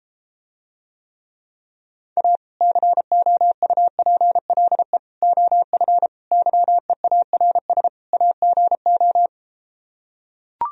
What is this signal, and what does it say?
Morse code, heard as a keyed tone, about 33 wpm, 700 Hz. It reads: A COUPLE OF YEARS AGO